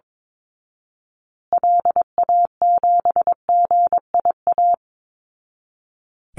Morse code, heard as a keyed tone, about 22 wpm, 700 Hz. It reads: LA7GIA
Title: Call Signs Repeated 2x - 22wpm